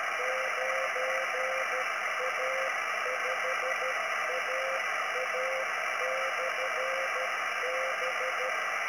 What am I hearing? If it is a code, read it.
Morse code, heard as a keyed tone, about 13 wpm, 520 Hz, with radio noise. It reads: 9A5AA/B